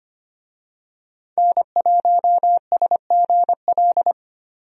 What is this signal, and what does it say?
Morse code, heard as a keyed tone, about 25 wpm, 700 Hz. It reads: N1SGL
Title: USA Call Signs - 5 Characters - 25wpm